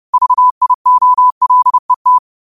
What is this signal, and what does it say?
Morse code, heard as a keyed tone, about 30 wpm, 1000 Hz. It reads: VIOLET
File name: qotdfiles / 30wpm